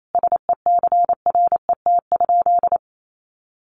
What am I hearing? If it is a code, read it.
Morse code, heard as a keyed tone, about 28 wpm, 700 Hz. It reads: SECRET?
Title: Sentences from Top 1200 Words 28wpm